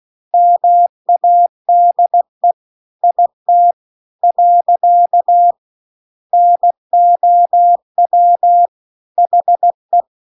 Morse code, heard as a keyed tone, about 16 wpm, 700 Hz. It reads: MADE IT . NOW HE